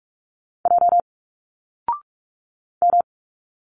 Morse code, heard as a keyed tone, about 46 wpm, 700 Hz. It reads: J M
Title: Single Letters 45wpm